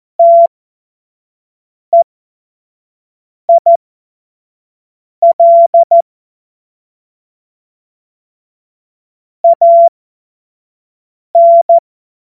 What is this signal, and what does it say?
Morse code, heard as a keyed tone, about 14 wpm, spaced as for 5 wpm, 680 Hz. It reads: TEIL AN
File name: Morsekurs / Witze / Witz 03 Juni 2025 25 F70